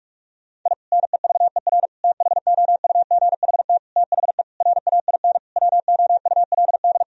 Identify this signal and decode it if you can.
Morse code, heard as a keyed tone, about 41 wpm, 700 Hz. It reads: I NEVER THOUGHT THE RAIN WOULD
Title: Sentences from Top 1000 Words - Review Part 1 - 45wpm